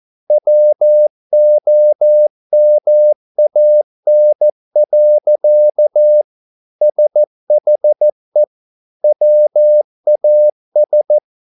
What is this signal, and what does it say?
Morse code, heard as a keyed tone, about 14 wpm, 600 Hz. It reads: WOMAN. SHE WAS